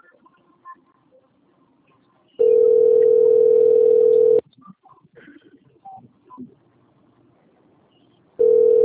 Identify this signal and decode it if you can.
Ringback tone (North America), 440 + 480 Hz.